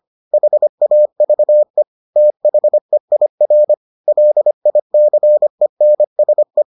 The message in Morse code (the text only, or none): HAVE THEIR LICENSE